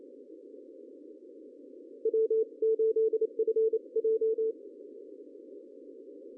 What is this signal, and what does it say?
Morse code, heard as a keyed tone, about 28 wpm, 440 Hz. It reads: W8FJ
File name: CW / cqww-cw-k5tr